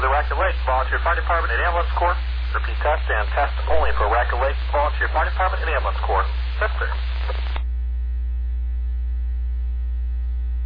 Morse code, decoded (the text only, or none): none